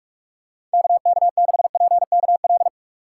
Morse code, heard as a keyed tone, about 47 wpm, 700 Hz. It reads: KK6PKL
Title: USA Call Signs - 6 Characters - 2x Character Spacing - 45wpm